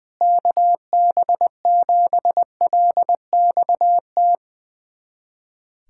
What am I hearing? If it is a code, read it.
Morse code, heard as a keyed tone, about 20 wpm, 700 Hz. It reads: KB7LXT
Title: USA Callsign DE Callsign - 6 Characters - 20wpm